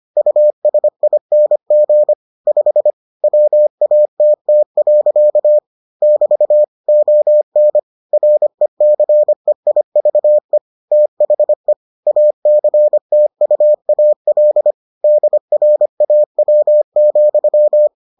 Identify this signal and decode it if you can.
Morse code, heard as a keyed tone, about 25 wpm, 600 Hz. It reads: USING 5 WATT. <BT> ON RECEIVE THE ACTUAL DRAW,